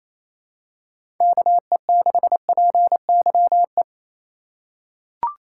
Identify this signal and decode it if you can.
Morse code, heard as a keyed tone, about 27 wpm, 700 Hz. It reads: KE6PYE
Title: USA Call Signs - 2x Character Spacing - 28wpm